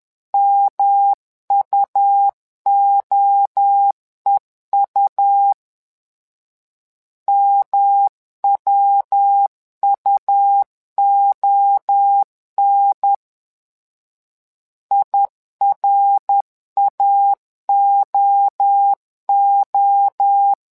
Morse code, heard as a keyed tone, about 10 wpm, 800 Hz. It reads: MUOEU MWUON IRAOO
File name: LEZIONI / 3 MWRO / 3 MWRO 10wpm 2148